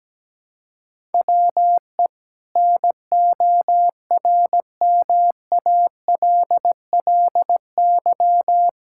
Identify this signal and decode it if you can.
Morse code, heard as a keyed tone, about 17 wpm, 700 Hz. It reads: WE NORMALLY
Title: Sentences from Top 1000 Words - Review Part 6 - 17wpm